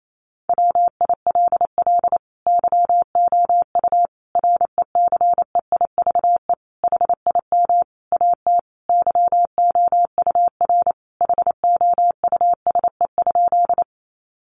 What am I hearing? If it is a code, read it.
Morse code, heard as a keyed tone, about 28 wpm, 700 Hz. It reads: WILL YOU RECEIVE HIM AT YOUR HOUSE?
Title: Sentences from Top 1000 Words 28wpm